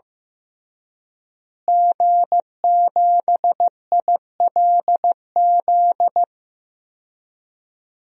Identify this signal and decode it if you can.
Morse code, heard as a keyed tone, about 15 wpm, 700 Hz. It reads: G7ILZ
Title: DXCC-100 Call Signs - 5 Characters - Repeated 3x - 15wpm